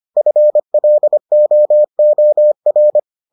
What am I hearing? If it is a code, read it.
Morse code, heard as a keyed tone, about 25 wpm, 600 Hz. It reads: FLOOR